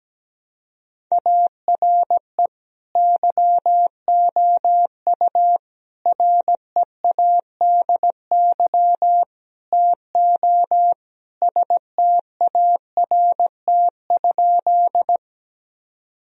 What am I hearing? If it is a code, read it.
Morse code, heard as a keyed tone, about 17 wpm, 700 Hz. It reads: ARE YOU READY TO START?
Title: Sentences from Top 1000 Words - Review Part 1 - 17wpm